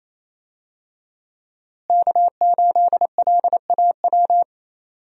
Morse code, heard as a keyed tone, about 28 wpm, 700 Hz. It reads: K8LAW
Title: USA Call Signs - 5 Characters - Repeated 3x - 28wpm